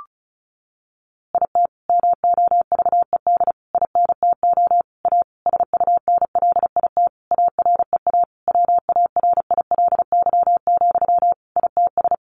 Morse code, heard as a keyed tone, about 34 wpm, 700 Hz. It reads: IT MOVED INTO A SUNLIT AREA WARILY, ITS
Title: A Sound Of Thunder 35wpm